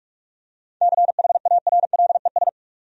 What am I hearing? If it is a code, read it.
Morse code, heard as a keyed tone, about 46 wpm, 700 Hz. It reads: CHARLES
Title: USA Common Names - 45wpm